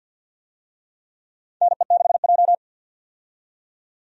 Morse code, heard as a keyed tone, about 53 wpm, 700 Hz. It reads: NE6J
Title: USA Call Signs - 4 Characters - 50wpm